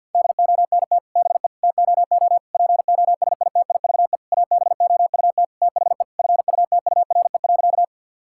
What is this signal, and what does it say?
Morse code, heard as a keyed tone, about 50 wpm, 700 Hz. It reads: DONT BE TOO POSITIVE ABOUT THE FUTURE.
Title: Sentences from Top 1000 Words - Review Part 5 - 50wpm